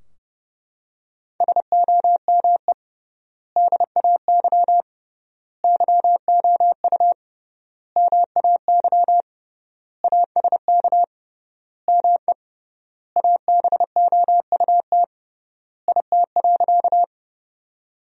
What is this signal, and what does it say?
Morse code, heard as a keyed tone, about 30 wpm, 700 Hz. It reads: SOME DAY YOU MAY ASK ME ABOUT IT.
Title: 3x Word-Spacing - Sentences from Top 200 Words - 30wpm